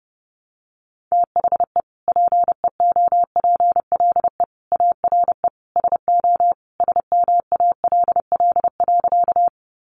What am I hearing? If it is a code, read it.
Morse code, heard as a keyed tone, about 30 wpm, 700 Hz. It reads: THE PEOPLE ARE SO SMALL.
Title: Additional Easy Phrases 30wpm